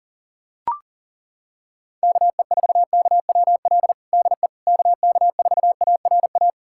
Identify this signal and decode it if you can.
Morse code, heard as a keyed tone, about 40 wpm, 700 Hz. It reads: KE4KWL DE KK4ARA
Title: USA Callsign DE Callsign - 6 Characters - 40wpm